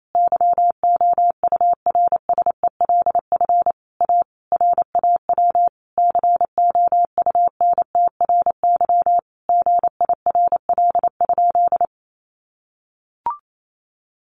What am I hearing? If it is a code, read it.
Morse code, heard as a keyed tone, about 28 wpm, 700 Hz. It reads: YOURSELF A RAW COUNTRY GIRL?
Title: Sentences from Top 1500 Words 28wpm